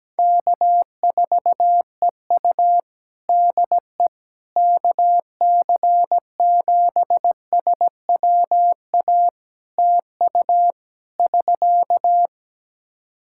Morse code, heard as a keyed tone, about 17 wpm, 700 Hz. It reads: K4EU DE KC7SWA TU <SK>
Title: Practice QSO - Exchange 4 - Ending - 17wpm